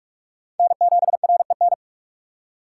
Morse code, heard as a keyed tone, about 55 wpm, 700 Hz. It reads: N7REN
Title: Speed Racing - USA 1x3 Call Signs 45wpm